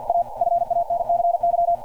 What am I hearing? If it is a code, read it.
Morse code, heard as a keyed tone, about 48 wpm, 690 Hz. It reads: A FANCY
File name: archived / CRSnet